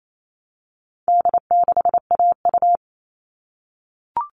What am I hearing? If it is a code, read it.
Morse code, heard as a keyed tone, about 28 wpm, 700 Hz. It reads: D6AU
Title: All Cut Numbers - 4-Digit 28wpm